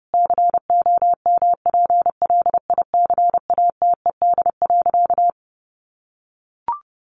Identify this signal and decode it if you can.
Morse code, heard as a keyed tone, about 30 wpm, 700 Hz. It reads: COMPLICATED.
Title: Sentences from Top 1200 Words 30wpm